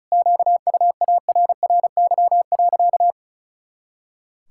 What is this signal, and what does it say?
Morse code, heard as a keyed tone, about 36 wpm, 700 Hz. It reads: QUARRY.
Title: Pangram Challenge v2 - 35wpm